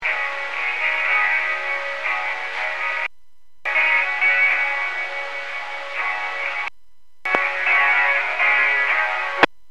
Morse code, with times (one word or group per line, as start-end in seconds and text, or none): none